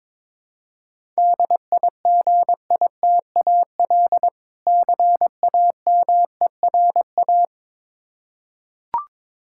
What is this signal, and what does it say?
Morse code, heard as a keyed tone, about 22 wpm, 700 Hz. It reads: DIGITAL CAMERA
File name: Difficult Words with Hints - 22wpm